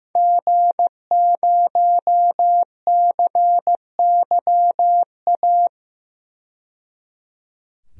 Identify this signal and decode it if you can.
Morse code, heard as a keyed tone, about 15 wpm, 700 Hz. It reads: G0CYA
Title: DXCC-100 Call Signs - 5 Characters - 15wpm